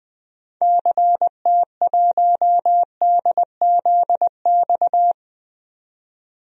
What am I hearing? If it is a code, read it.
Morse code, heard as a keyed tone, about 20 wpm, 700 Hz. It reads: CT1DZX
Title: DXCC-100 Call Signs - 6 Characters - 20wpm